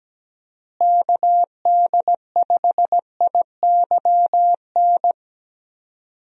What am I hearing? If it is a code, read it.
Morse code, heard as a keyed tone, about 17 wpm, 700 Hz. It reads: KD5IYN
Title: USA Call Signs - 6 Characters - Repeated 3x - 17wpm